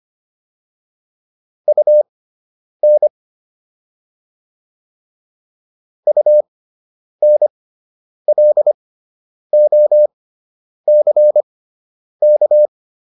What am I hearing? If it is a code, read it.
Morse code, heard as a keyed tone, about 25 wpm, spaced as for 9 wpm, 600 Hz. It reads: UN UNLOCK